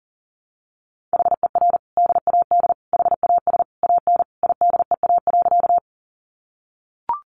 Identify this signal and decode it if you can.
Morse code, heard as a keyed tone, about 40 wpm, 700 Hz. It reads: HER DAD HAS AN IDEA.
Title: Additional Easy Phrases 40wpm